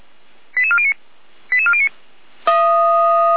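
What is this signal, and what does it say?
Five-tone selective call (ZVEI), heard as quick runs of five tones, tones of about 70 ms. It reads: 88409 88409
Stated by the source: Five-tone ZVEI selective call 88291, received by SDR.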